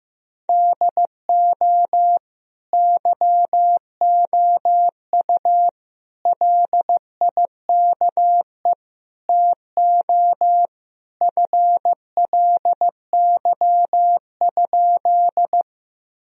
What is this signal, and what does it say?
Morse code, heard as a keyed tone, about 15 wpm, 700 Hz. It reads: DO YOU LIKE TO FLY?
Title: Sentences from Top 1000 Words - Review Part 2 - 15wpm